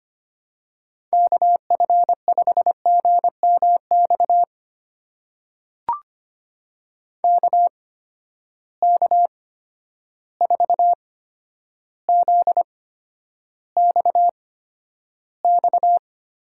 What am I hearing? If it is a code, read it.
Morse code, heard as a keyed tone, about 25 wpm, spaced as for 11 wpm, 700 Hz. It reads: KF5GMX KK4ZXX
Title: USA Call Signs - 6 Characters - 8x Character Spacing - 25wpm